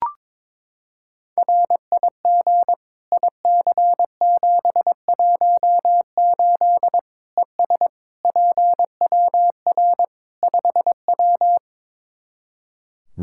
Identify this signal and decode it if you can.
Morse code, heard as a keyed tone, about 22 wpm, 700 Hz. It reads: RIG IC718 ES PWR 5W
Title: K1USN SST QSO Practice 22wpm